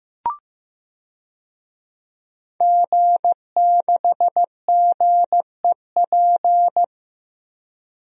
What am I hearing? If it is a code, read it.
Morse code, heard as a keyed tone, about 16 wpm, 700 Hz. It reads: G6GEP
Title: DXCC-100 Call Signs - 5 Characters - 15wpm